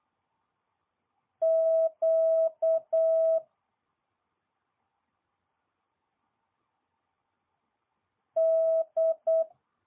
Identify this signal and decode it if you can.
Morse code, heard as a keyed tone, about 8 wpm, 650 Hz. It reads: Q D
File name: jh8cbh / 2022cw